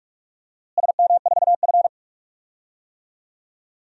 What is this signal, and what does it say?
Morse code, heard as a keyed tone, about 43 wpm, 700 Hz. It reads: SM4F
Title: DXCC-100 Call Signs - 4 Characters - Repeated 3x - 45wpm